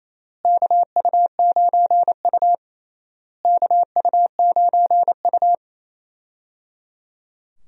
Morse code, heard as a keyed tone, about 28 wpm, 700 Hz. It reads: KU9U KU9U
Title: USA Call Signs - 4 Characters - Repeated 3x - 28wpm